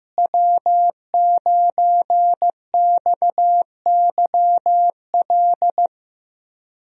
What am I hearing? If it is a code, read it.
Morse code, heard as a keyed tone, about 15 wpm, 700 Hz. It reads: W9XYL
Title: USA Call Signs - 5 Characters - Repeated 2x - 15wpm